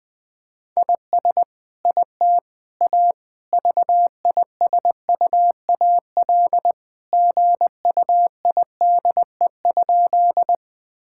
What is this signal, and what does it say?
Morse code, heard as a keyed tone, about 21 wpm, 700 Hz. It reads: IS IT A VISUAL GUIDE?
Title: Sentences from Top 1000 Words - Review Part 6 - 20wpm